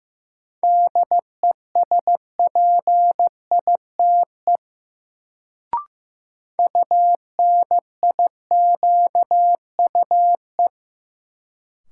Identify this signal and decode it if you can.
Morse code, heard as a keyed tone, about 15 wpm, 700 Hz. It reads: DESPITE UNIQUE
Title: Sentences from Top 1000 Words - Review Part 7 - New Word Drill - 15wpm